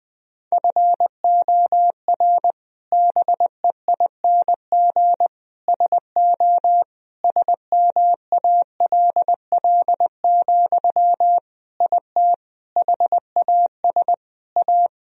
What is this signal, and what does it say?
Morse code, heard as a keyed tone, about 21 wpm, 700 Hz. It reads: FOR BEING SO SMALL, IT HAS A
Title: Sentences from Top 1000 Words - Review Part 3 - 20wpm